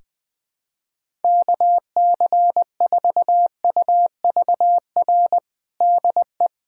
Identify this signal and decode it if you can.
Morse code, heard as a keyed tone, about 20 wpm, 700 Hz. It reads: KC4UVR DE
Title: USA Callsign DE Callsign 20wpm